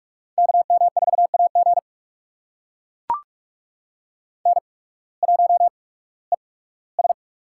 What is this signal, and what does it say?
Morse code, heard as a keyed tone, about 44 wpm, spaced as for 21 wpm, 700 Hz. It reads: KM4AG N1ES
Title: USA Call Signs - 8x Character Spacing - 45wpm